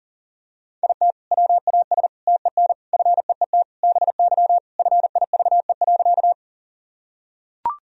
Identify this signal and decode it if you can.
Morse code, heard as a keyed tone, about 37 wpm, 700 Hz. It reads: IT WAS TEN FEET BY FIVE.
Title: Sentences from Top 1000 Words - Review Part 2 - 40wpm